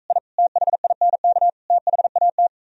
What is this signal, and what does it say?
Morse code, heard as a keyed tone, about 45 wpm, 700 Hz. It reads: I THINK THAT